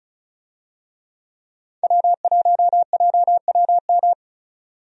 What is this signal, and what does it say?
Morse code, heard as a keyed tone, about 36 wpm, 700 Hz. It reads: W1JWM
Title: USA Call Signs - 5 Characters - 35wpm